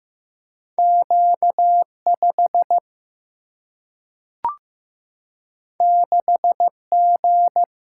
Morse code, heard as a keyed tone, about 15 wpm, 700 Hz. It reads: Q 5 6 G
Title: Single-Letter Single-Number 15wpm